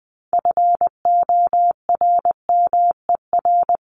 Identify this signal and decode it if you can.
Morse code, heard as a keyed tone, about 20 wpm, 700 Hz. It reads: FORMER
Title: LY Suffix - Top 100 Words with Base Words 20wpm